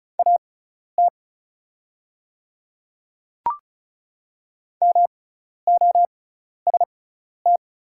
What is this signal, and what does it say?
Morse code, heard as a keyed tone, about 36 wpm, spaced as for 12 wpm, 700 Hz. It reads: AT MOST
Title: 6x Character Spacing - Top 300 Words - 35wpm